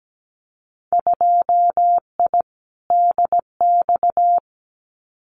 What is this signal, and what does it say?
Morse code, heard as a keyed tone, about 17 wpm, 700 Hz. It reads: 2I DX
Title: Winter Field Day Exchanges 17wpm